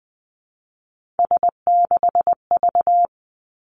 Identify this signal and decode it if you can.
Morse code, heard as a keyed tone, about 19 wpm, 700 Hz. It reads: S6V
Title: Three Challenging Letter-Number Combinations 20wpm